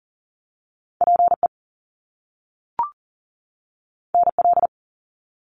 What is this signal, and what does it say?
Morse code, heard as a keyed tone, about 40 wpm, 700 Hz. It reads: PE NL
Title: Canadian Provinces and Territories 40wpm